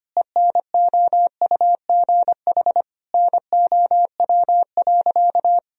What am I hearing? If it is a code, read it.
Morse code, heard as a keyed tone, about 25 wpm, 700 Hz. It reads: ENOUGH NOW.
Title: Sentences from Top 1000 Words - Review Part 2 - 25wpm